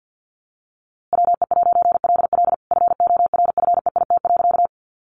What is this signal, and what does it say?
Morse code, heard as a keyed tone, about 50 wpm, 700 Hz. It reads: WE'LL FORFEIT.
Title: A Sound Of Thunder 50wpm